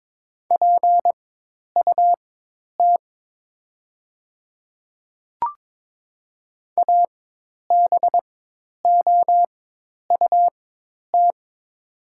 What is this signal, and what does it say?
Morse code, heard as a keyed tone, about 22 wpm, spaced as for 10 wpm, 700 Hz. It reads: PUT ABOUT
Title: 4x Character Spacing - Top 300 Words - 22wpm